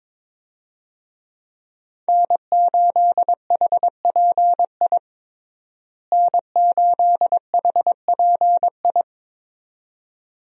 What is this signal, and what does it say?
Morse code, heard as a keyed tone, about 21 wpm, 700 Hz. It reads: N8HPI N8HPI
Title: USA Call Signs - 5 Characters - Repeated 2x - 22wpm